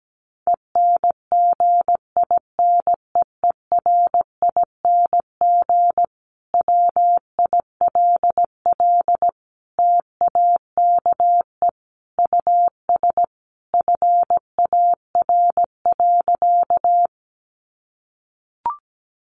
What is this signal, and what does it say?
Morse code, heard as a keyed tone, about 17 wpm, 700 Hz. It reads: ENGINEERING WILL TAKE US FAR.
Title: Sentences from Top 1500 Words 17wpm